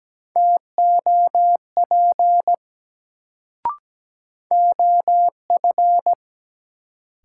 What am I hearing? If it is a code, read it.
Morse code, heard as a keyed tone, about 17 wpm, 700 Hz. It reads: TOP OF
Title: Sets of 2 Words - New Word Drill - 17wpm